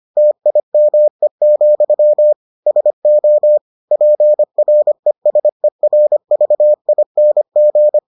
Morse code, heard as a keyed tone, about 25 wpm, 600 Hz. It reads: TIME, SO PRESERVING